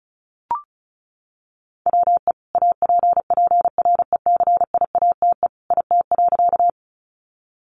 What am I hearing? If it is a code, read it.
Morse code, heard as a keyed tone, about 34 wpm, 700 Hz. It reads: WE APPRECIATE IT.
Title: Sentences from Top 1300 Words 35wpm